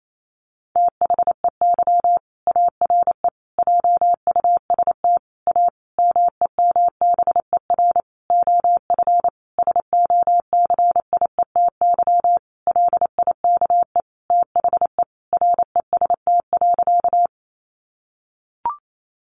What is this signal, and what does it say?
Morse code, heard as a keyed tone, about 28 wpm, 700 Hz. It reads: THEY ARE JUST A MEMBER OF SOCIETY LIKE THE REST.
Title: Sentences from Top 1100 Words 28wpm